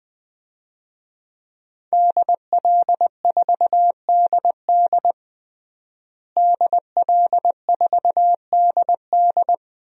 Morse code, heard as a keyed tone, about 21 wpm, 700 Hz. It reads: DL4DD DL4DD
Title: DXCC-100 Call Signs - 5 Characters - Repeated 2x - 20wpm